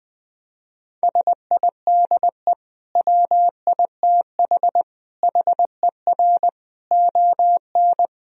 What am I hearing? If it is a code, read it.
Morse code, heard as a keyed tone, about 21 wpm, 700 Hz. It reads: SIDE WITH HER ON
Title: Sentences from Top 1000 Words - Review Part 1 - 20wpm